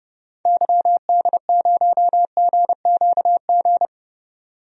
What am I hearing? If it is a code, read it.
Morse code, heard as a keyed tone, about 32 wpm, 700 Hz. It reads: YD0GQG
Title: DXCC-100 Call Signs - 6 Characters - 30wpm